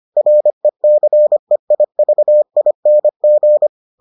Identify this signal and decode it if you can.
Morse code, heard as a keyed tone, about 25 wpm, 600 Hz. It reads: RECEIVING